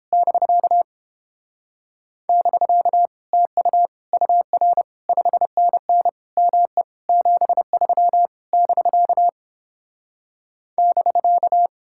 Morse code, heard as a keyed tone, about 31 wpm, 700 Hz. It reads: <BK> <BK> TU UR 5NN ME 73 <BK> <BK>